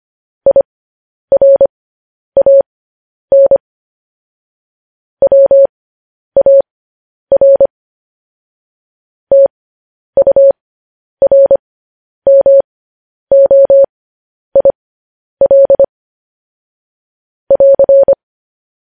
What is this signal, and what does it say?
Morse code, heard as a keyed tone, about 25 wpm, spaced as for 10 wpm, 550 Hz. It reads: IRAN WAR TURMOIL <AR>
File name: podcast / News headlines-1x-25x10